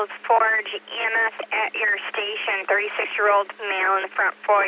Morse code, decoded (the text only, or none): none